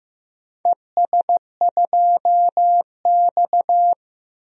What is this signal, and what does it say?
Morse code, heard as a keyed tone, about 16 wpm, 700 Hz. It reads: ES2X
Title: DXCC-100 Call Signs - 4 Characters - Repeated 2x - 15wpm